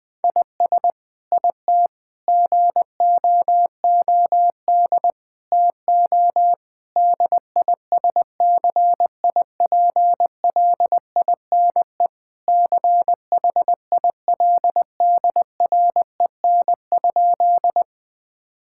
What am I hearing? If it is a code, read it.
Morse code, heard as a keyed tone, about 21 wpm, 700 Hz. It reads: IS IT GOOD TO DISCIPLINE CHILDREN?
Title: Sentences from Top 1000 Words - Review Part 6 - 20wpm